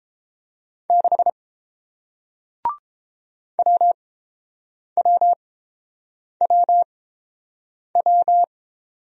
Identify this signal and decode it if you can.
Morse code, heard as a keyed tone, about 31 wpm, spaced as for 7 wpm, 690 Hz. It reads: 6 WWWW